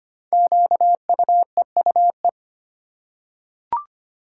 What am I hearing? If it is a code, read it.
Morse code, heard as a keyed tone, about 25 wpm, 700 Hz. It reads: QUEUE
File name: 5 Letter Words - First 3 Letters Given - 25wpm